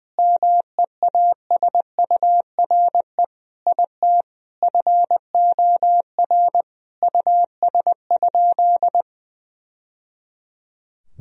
Morse code, heard as a keyed tone, about 21 wpm, 700 Hz. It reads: MEASURE IT FOR US?
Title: Sentences from Top 1000 Words - Review Part 2 - 20wpm